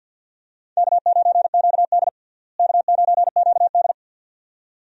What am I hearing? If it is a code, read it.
Morse code, heard as a keyed tone, about 50 wpm, 700 Hz. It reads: K9QD K9QD
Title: USA Call Signs - 4 Characters - Repeated 2x - 50wpm